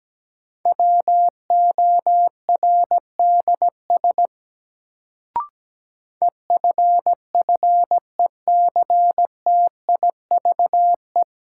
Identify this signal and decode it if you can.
Morse code, heard as a keyed tone, about 17 wpm, 700 Hz. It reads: WORDS EFFECTIVE
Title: Sentences from Top 1000 Words - Review Part 6 - New Word Drill - 17wpm